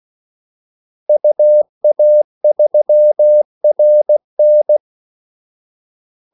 Morse code, heard as a keyed tone, about 16 wpm, 600 Hz. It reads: UA3RN
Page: Kaldesignaler 16 wpm